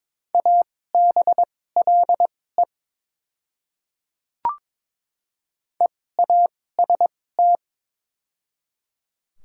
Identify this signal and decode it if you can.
Morse code, heard as a keyed tone, about 22 wpm, spaced as for 16 wpm, 700 Hz. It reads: ABLE EAST
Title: 2x Character Spacing - Top 500 Words - 22wpm